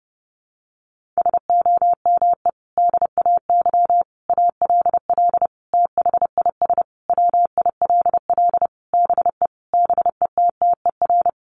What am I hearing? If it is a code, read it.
Morse code, heard as a keyed tone, about 30 wpm, 700 Hz. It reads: SOME DAY ALL THIS WILL BE BETTER
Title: Sentences from Top 1300 Words 30wpm